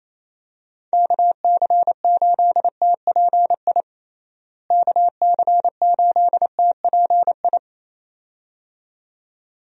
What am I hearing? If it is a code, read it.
Morse code, heard as a keyed tone, about 28 wpm, 700 Hz. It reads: KC8TPI KC8TPI
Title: USA Call Signs - 6 Characters - Repeated 2x - 28wpm